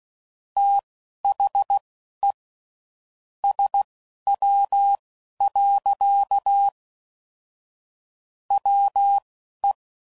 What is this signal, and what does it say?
Morse code, heard as a keyed tone, about 16 wpm, spaced as for 11 wpm, 800 Hz. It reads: THE SW. WE
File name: CW Code Groups @10WPM / Weather Sets / Weather 5 5